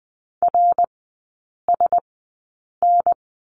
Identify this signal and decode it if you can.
Morse code, heard as a keyed tone, about 20 wpm, spaced as for 8 wpm, 700 Hz. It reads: RSN
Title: 074 Lesson 13e - Letter L - Three Characters 20wpm